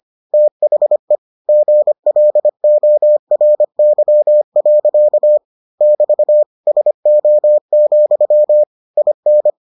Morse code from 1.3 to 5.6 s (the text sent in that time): GLORY.